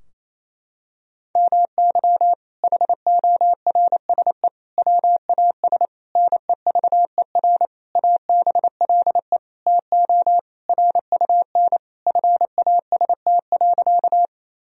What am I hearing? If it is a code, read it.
Morse code, heard as a keyed tone, about 28 wpm, 700 Hz. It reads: MY HORSE WAS NEVER ABLE TO RUN FAST.
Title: Sentences from Top 1000 Words - Review Part 2 - 28wpm